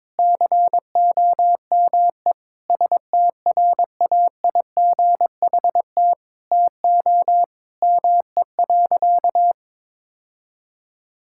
Morse code, heard as a keyed tone, about 22 wpm, 700 Hz. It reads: COME STRAIGHT TO ME.
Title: Sentences from Top 1000 Words - Review Part 6 - 22wpm